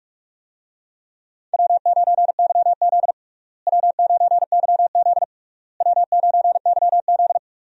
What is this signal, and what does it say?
Morse code, heard as a keyed tone, about 46 wpm, 700 Hz. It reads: W9YZ W9YZ W9YZ
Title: USA Call Signs - 4 Characters - Repeated 3x - 45wpm